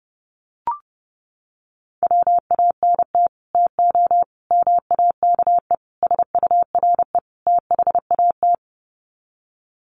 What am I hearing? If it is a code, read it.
Morse code, heard as a keyed tone, about 30 wpm, 700 Hz. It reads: WANT TO MAKE SURE THAT
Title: Sets of 5 Words 30wpm